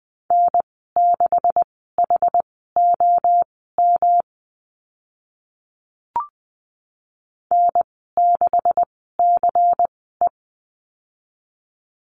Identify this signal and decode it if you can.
Morse code, heard as a keyed tone, about 20 wpm, spaced as for 15 wpm, 700 Hz. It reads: N6HOM N6CE
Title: 140 Lesson 23g - Letter f - Call Signs 20wpm